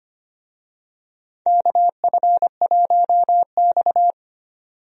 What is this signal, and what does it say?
Morse code, heard as a keyed tone, about 25 wpm, 700 Hz. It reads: KF1X
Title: USA Call Signs - 4 Characters - 25wpm